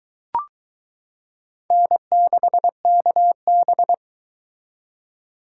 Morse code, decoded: N6KB

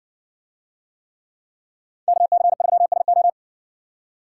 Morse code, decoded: KC3SO

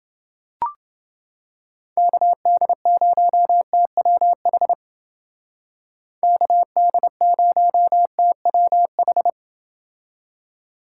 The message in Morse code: KD0TWH KD0TWH

